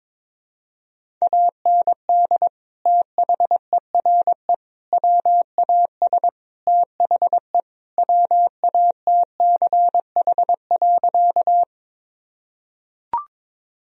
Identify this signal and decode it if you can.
Morse code, heard as a keyed tone, about 22 wpm, 700 Hz. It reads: AND THERE WAS THE WATCH.
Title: Sentences from Top 1000 Words - Review Part 1 - 22wpm